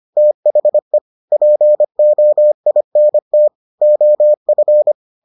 Morse code, heard as a keyed tone, about 25 wpm, 600 Hz. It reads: THE POINT OF